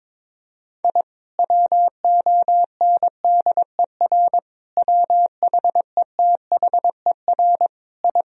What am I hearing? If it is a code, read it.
Morse code, heard as a keyed tone, about 22 wpm, 700 Hz. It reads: I WONDER WHETHER I
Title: Sentences from Top 1000 Words - Review Part 3 - 22wpm